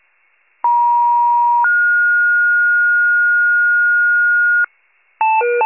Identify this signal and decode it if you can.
Two-tone page, heard as a one-second tone, then a three-second tone, about 950 then 1430 Hz, with noise. It is heard from 0.6 to 4.7 s.
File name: pager / 76 FIRE / wcalert